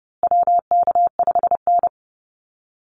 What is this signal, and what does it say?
Morse code, heard as a keyed tone, about 30 wpm, 700 Hz. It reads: WK5N
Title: Call Signs Repeated 2x - 30wpm